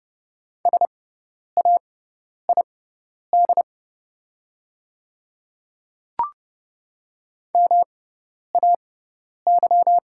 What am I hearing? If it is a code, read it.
Morse code, heard as a keyed tone, about 29 wpm, spaced as for 10 wpm, 690 Hz. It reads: SAID MAY
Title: 6x Character Spacing - Top 100 Words - 30wpm